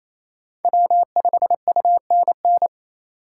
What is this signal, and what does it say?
Morse code, heard as a keyed tone, about 28 wpm, 700 Hz. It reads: W5UNN
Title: USA Call Signs - 5 Characters - 4x Character Spacing - 28wpm